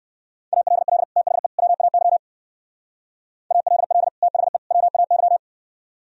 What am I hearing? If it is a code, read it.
Morse code, heard as a keyed tone, about 62 wpm, 700 Hz. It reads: ALL THE WAY ALL THE WAY